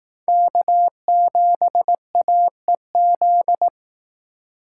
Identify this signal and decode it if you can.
Morse code, heard as a keyed tone, about 18 wpm, 700 Hz. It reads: K7AEZ
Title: Speed Racing - USA Call Signs 15wpm